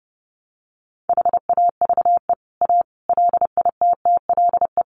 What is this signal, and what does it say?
Morse code, heard as a keyed tone, about 30 wpm, 700 Hz. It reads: HAVE A LITTLE
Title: Sentences from Top 1300 Words 30wpm